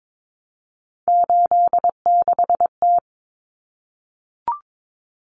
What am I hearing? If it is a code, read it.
Morse code, heard as a keyed tone, about 22 wpm, 700 Hz. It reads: 86T
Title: Common Cut Numbers - 3-Digit 22wpm